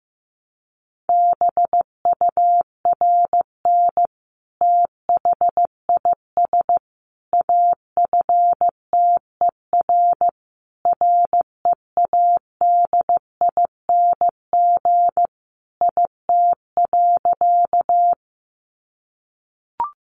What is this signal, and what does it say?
Morse code, heard as a keyed tone, about 15 wpm, 700 Hz. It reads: BURN THIS AFTER READING IT.
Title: Sentences from Top 1300 Words 15wpm